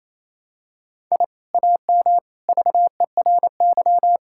Morse code, heard as a keyed tone, about 28 wpm, 700 Hz. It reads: I AM VERY
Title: Sentences from Top 1000 Words - Review Part 6 - 28wpm